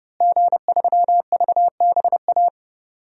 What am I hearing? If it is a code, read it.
Morse code, heard as a keyed tone, about 30 wpm, 700 Hz. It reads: G3VBA
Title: DXCC-100 Call Signs - 5 Characters - Repeated 3x - 30wpm